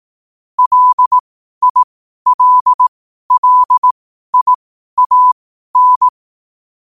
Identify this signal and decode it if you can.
Morse code, heard as a keyed tone, about 18 wpm, spaced as for 12 wpm, 1000 Hz. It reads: LILLIAN
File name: qotdfiles / 12wpm